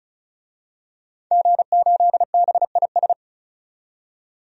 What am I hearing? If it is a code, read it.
Morse code, heard as a keyed tone, about 36 wpm, 700 Hz. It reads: G8BIS
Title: DXCC-100 Call Signs - 5 Characters - 35wpm